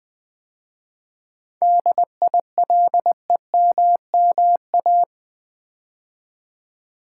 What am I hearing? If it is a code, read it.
Morse code, heard as a keyed tone, about 20 wpm, 700 Hz. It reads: DILEMMA
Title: 7 Letter Words - First 3 Letters Given - 20wpm